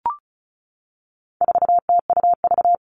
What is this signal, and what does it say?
Morse code, heard as a keyed tone, about 34 wpm, 700 Hz. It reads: 4TUV
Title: All Cut Numbers - 4-Digit 35wpm